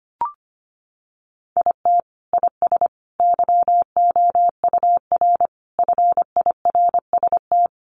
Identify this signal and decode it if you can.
Morse code, heard as a keyed tone, about 25 wpm, 700 Hz. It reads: IT IS YOUR FIRST
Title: Sentences from Top 1100 Words 25wpm